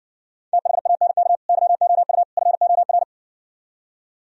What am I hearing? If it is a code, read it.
Morse code, heard as a keyed tone, about 59 wpm, 700 Hz. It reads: THANK YOU FOR